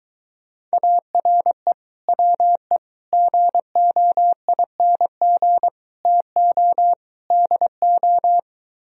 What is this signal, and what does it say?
Morse code, heard as a keyed tone, about 23 wpm, 700 Hz. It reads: ARE WE GOING TO DO